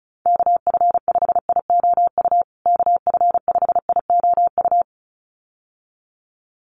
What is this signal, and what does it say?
Morse code, heard as a keyed tone, about 34 wpm, 700 Hz. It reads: KF5IOU KF5IOU
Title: USA Call Signs Repeated 2x 35wpm